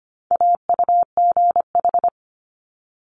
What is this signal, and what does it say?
Morse code, heard as a keyed tone, about 25 wpm, 700 Hz. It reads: AUGH
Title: Multi-letter Phonemes Quick Fire 25wpm